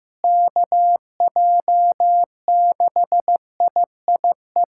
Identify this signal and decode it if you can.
Morse code, heard as a keyed tone, about 15 wpm, 700 Hz. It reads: KJ6IIE
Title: USA Callsign DE Callsign - 6 Characters - 15wpm